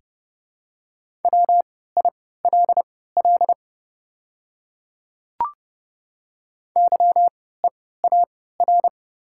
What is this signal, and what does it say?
Morse code, heard as a keyed tone, about 31 wpm, spaced as for 17 wpm, 700 Hz. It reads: WILL YEAR
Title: 3x Character Spacing - Top 300 Words - 30wpm